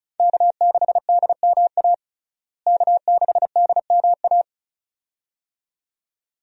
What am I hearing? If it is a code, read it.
Morse code, heard as a keyed tone, about 36 wpm, 700 Hz. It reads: K6DMA K6DMA
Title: USA Call Signs - 5 Characters - Repeated 2x - 35wpm